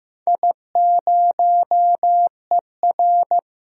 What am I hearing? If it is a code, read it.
Morse code, heard as a keyed tone, about 15 wpm, 700 Hz. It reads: I0ER
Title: DXCC-100 Call Signs - 4 Characters - Repeated 2x - 15wpm